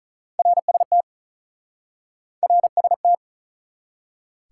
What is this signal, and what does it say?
Morse code, heard as a keyed tone, about 46 wpm, 700 Hz. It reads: RST RST